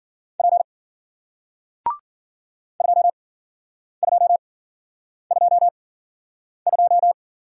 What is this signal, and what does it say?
Morse code, heard as a keyed tone, about 53 wpm, spaced as for 9 wpm, 700 Hz. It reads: P 2222